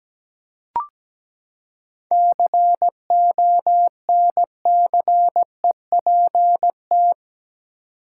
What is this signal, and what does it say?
Morse code, heard as a keyed tone, about 17 wpm, 700 Hz. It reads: CONCEPT
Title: Seven Letter Words - 17wpm